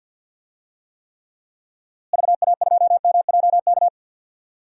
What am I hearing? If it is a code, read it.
Morse code, heard as a keyed tone, about 51 wpm, 700 Hz. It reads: VA2MJK
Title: DXCC-100 Call Signs - 6 Characters - Repeated 3x - 50wpm